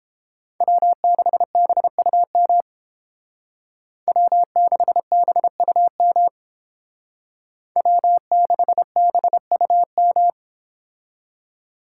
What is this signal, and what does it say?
Morse code, heard as a keyed tone, about 28 wpm, 700 Hz. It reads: W6BUM W6BUM W6BUM